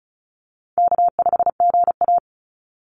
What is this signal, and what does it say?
Morse code, heard as a keyed tone, about 34 wpm, 700 Hz. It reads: K5GA
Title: Easy Call Signs Repeated 2x - 35wpm